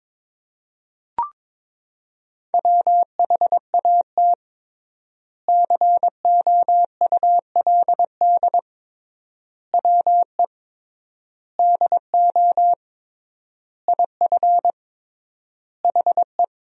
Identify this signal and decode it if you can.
Morse code, heard as a keyed tone, about 22 wpm, 700 Hz. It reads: WHAT COULD WE DO IF HE
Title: 3x Word-Spacing - Sentences from Top 400 Words - 22wpm